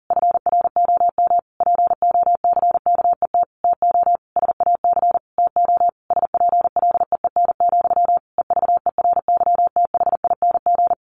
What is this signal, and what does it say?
Morse code, heard as a keyed tone, about 40 wpm, 700 Hz. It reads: FROM POCKET TO SAC TO SPLEEN, EVERYTHING
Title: A Sound Of Thunder 40wpm